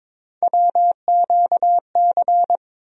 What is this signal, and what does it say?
Morse code, heard as a keyed tone, about 21 wpm, 700 Hz. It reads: WQC
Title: USA Call Sign Postfixes - 3 Letters 22wpm